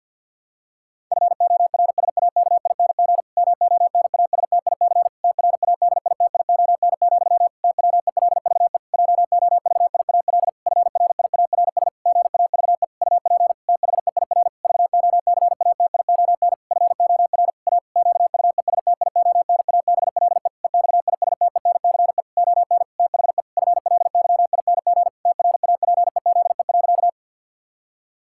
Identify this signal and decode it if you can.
Morse code, heard as a keyed tone, about 50 wpm, 700 Hz. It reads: FORSAKING MONASTIC TRADITION, TWELVE JOVIAL FRIARS GAVE UP THEIR VOCATION FOR A QUESTIONABLE EXISTENCE ON THE FLYING TRAPEZE.
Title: Pangram Challenge v2 - 50wpm